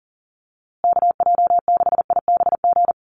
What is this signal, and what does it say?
Morse code, heard as a keyed tone, about 40 wpm, 700 Hz. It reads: KJ6IBG
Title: USA Call Signs Repeated 2x 40wpm